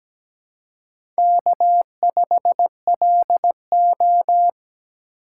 Morse code, heard as a keyed tone, about 17 wpm, 700 Hz. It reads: K5LO
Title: USA Call Signs - 3x Character Spacing - 17wpm